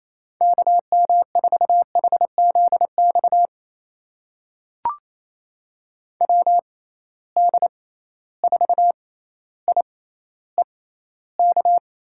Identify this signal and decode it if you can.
Morse code, heard as a keyed tone, about 28 wpm, spaced as for 14 wpm, 700 Hz. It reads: KM4HZX WD4IEK
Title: USA Call Signs - 6 Characters - 6x Character Spacing - 28wpm